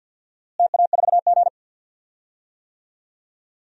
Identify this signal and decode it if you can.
Morse code, heard as a keyed tone, about 48 wpm, 700 Hz. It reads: TA4G
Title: DXCC-100 Call Signs - 4 Characters - Repeated 2x - 50wpm